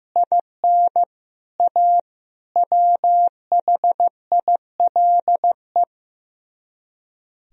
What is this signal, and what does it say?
Morse code, heard as a keyed tone, about 15 wpm, 700 Hz. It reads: IN A WHILE
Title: Speed-Racing - Sets of 5 Words 15wpm